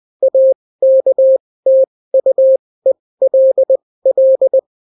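Morse code, heard as a keyed tone, about 20 wpm, spaced as for 16 wpm, 530 Hz. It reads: AKTUELL
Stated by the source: Daily Morse weather and top stories at 16 wpm, clean without QRN.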